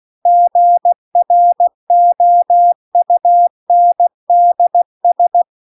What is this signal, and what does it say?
Morse code, heard as a keyed tone, about 16 wpm, 700 Hz. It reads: GROUNDS